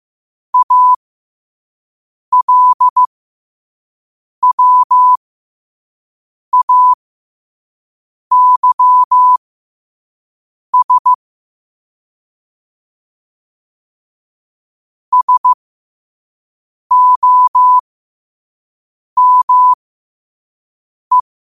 Morse code, heard as a keyed tone, about 15 wpm, spaced as for 5 wpm, 1000 Hz. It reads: ALWAYS SOME
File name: qotdfiles / 5wpm